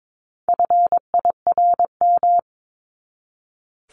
Morse code, heard as a keyed tone, about 22 wpm, 700 Hz. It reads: FIRM
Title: Top 1251-1500 Words 22wpm